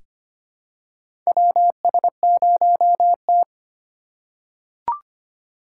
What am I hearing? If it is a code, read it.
Morse code, heard as a keyed tone, about 25 wpm, 700 Hz. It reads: WS0T
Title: USA Call Signs - 4 Characters - 4x Character Spacing - 25wpm